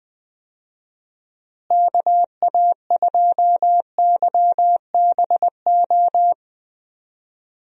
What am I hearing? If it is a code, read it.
Morse code, heard as a keyed tone, about 20 wpm, 700 Hz. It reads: KA2YBO
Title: USA Call Signs - 6 Characters - Repeated 2x - 20wpm